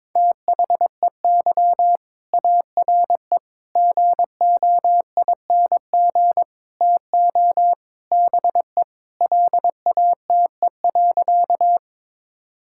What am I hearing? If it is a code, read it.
Morse code, heard as a keyed tone, about 22 wpm, 700 Hz. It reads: THEY ARE GOING TO BE LATE.
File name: practice / 22wpm / Sentences with 4-Word Phrases - 22wpm